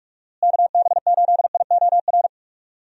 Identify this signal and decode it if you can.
Morse code, heard as a keyed tone, about 46 wpm, 700 Hz. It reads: KB8IOR
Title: USA Call Signs - 6 Characters - Repeated 3x - 45wpm